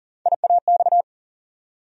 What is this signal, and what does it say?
Morse code, heard as a keyed tone, about 40 wpm, 700 Hz. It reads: IAX
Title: USA Call Sign Postfixes - 3 Letters 40wpm